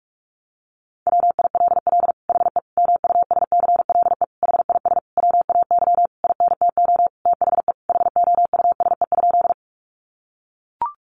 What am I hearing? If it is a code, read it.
Morse code, heard as a keyed tone, about 44 wpm, 700 Hz. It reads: WILL HE MUSCLE HIS WAY INTO THE HOUSE?
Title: Sentences from Top 1300 Words 45wpm